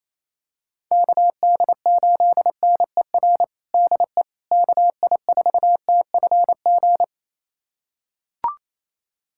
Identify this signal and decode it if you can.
Morse code, heard as a keyed tone, about 28 wpm, 700 Hz. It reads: KD8NER DE KI4TFG
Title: USA Callsign DE Callsign - 6 Characters - 28wpm